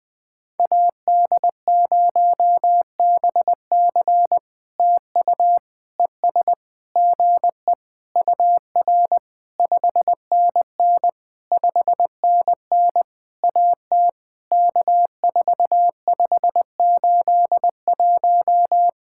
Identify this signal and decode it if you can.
Morse code, heard as a keyed tone, about 20 wpm, 700 Hz. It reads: AD0BC TU ES GE UR 5NN 5NN AT K4581
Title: POTA QSO Practice - 20wpm